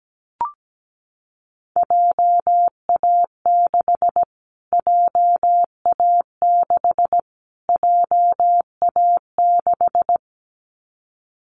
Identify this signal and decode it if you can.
Morse code, heard as a keyed tone, about 17 wpm, 700 Hz. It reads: JA6 JA6 JA6
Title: Callsign Prefixes for Top 25 DXCC - Repeated 3x - 17wpm